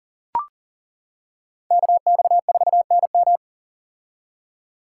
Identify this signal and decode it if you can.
Morse code, heard as a keyed tone, about 40 wpm, 700 Hz. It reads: KX4NM
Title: K1USN SST QSO Practice - 40wpm